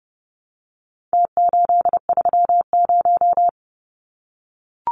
Morse code, heard as a keyed tone, about 30 wpm, 700 Hz. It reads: T830
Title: Numbers - Contest Style - 4-Digit 30wpm